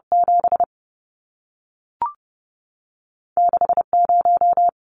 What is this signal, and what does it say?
Morse code, heard as a keyed tone, about 30 wpm, spaced as for 3 wpm, 700 Hz. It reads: 76 0
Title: Numbers - 2-Digit 30wpm